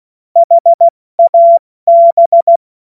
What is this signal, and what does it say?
Morse code, heard as a keyed tone, about 16 wpm, 680 Hz. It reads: HAB
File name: Morsekurs / Witze / Witz 02 April 2024 70 F80